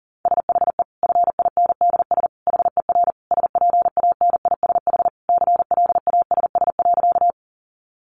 Morse code, heard as a keyed tone, about 40 wpm, 700 Hz. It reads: SHE FINDS HER SPANISH CLASS.
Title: Additional Easy Phrases 40wpm